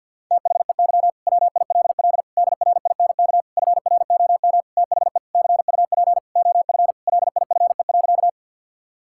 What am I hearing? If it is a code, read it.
Morse code, heard as a keyed tone, about 50 wpm, 700 Hz. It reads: THEY WILL DRINK FROM THE CUP OF LIFE.
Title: Sentences from Top 1600 Words 50wpm